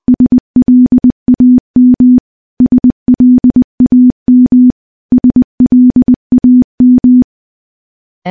Morse code, heard as a keyed tone, about 20 wpm, 250 Hz. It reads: SLAM SLAM SLAM
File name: morse / morse-training-20WPM